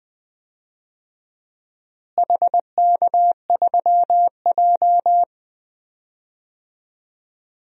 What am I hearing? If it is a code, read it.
Morse code, heard as a keyed tone, about 21 wpm, 700 Hz. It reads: HK3J
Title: DXCC-100 Call Signs - 4 Characters - Repeated 2x - 20wpm